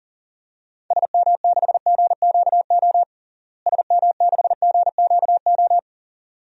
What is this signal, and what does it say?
Morse code, heard as a keyed tone, about 40 wpm, 700 Hz. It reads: SM6GQO SM6GQO
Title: DXCC-100 Call Signs - 6 Characters - Repeated 2x - 40wpm